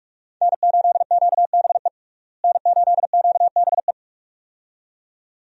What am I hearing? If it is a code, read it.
Morse code, heard as a keyed tone, about 41 wpm, 700 Hz. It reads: N8QBE N8QBE
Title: USA Call Signs - 5 Characters - Repeated 2x - 45wpm